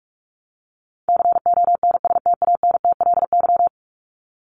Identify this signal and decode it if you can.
Morse code, heard as a keyed tone, about 44 wpm, 700 Hz. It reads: CONSTANTLY
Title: Top 751-1000 Words 45wpm